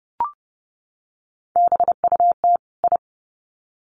Morse code, heard as a keyed tone, about 30 wpm, 700 Hz. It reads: BUT I
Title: Sets of 2 Words 30wpm